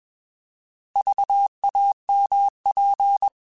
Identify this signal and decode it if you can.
Morse code, heard as a keyed tone, about 21 wpm, 800 Hz. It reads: VAMP
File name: Alphabetical / Alphabetical mp3s / Vs @20